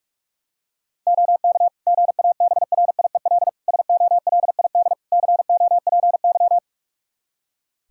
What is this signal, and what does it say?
Morse code, heard as a keyed tone, about 46 wpm, 700 Hz. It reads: OK GABRIEL SOLID COPY
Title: Practice QSO - Exchange 3 - More Info - 45wpm